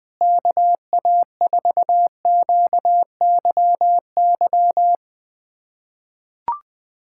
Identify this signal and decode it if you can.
Morse code, heard as a keyed tone, about 21 wpm, 700 Hz. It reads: KA4QYY
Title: USA Callsign DE Callsign 20wpm